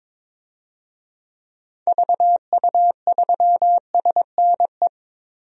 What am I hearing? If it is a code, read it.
Morse code, heard as a keyed tone, about 22 wpm, 700 Hz. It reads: VU3SNE
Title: DXCC-100 Call Signs - 6 Characters - 22wpm